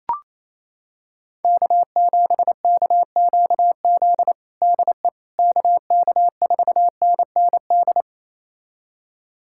Morse code, heard as a keyed tone, about 28 wpm, 700 Hz. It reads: K7KQZ DE KK4NND
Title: USA Callsign DE Callsign 28wpm